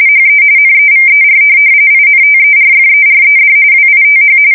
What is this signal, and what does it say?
RTTY, about 45.45 baud. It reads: HE NEVER INTERFERED WITH
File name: fsk170 / USS-New-Jersey